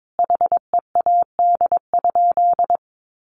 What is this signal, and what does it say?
Morse code, heard as a keyed tone, about 22 wpm, 700 Hz. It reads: HEAD?
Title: Sentences from Top 600 Words 22wpm